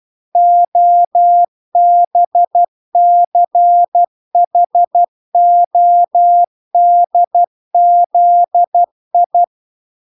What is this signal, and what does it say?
Morse code, heard as a keyed tone, about 12 wpm, 700 Hz. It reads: OBCHODZI